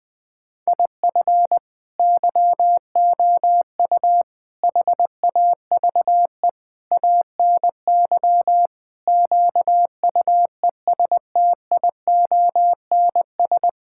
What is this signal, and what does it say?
Morse code, heard as a keyed tone, about 20 wpm, 700 Hz. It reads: IF YOU HAVE ANY QUESTIONS
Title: Sets of 8 Words - 20wpm